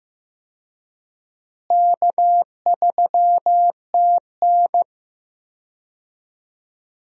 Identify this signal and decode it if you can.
Morse code, heard as a keyed tone, about 15 wpm, 700 Hz. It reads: K3TN
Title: USA Call Signs - 4 Characters - Repeated 3x - 15wpm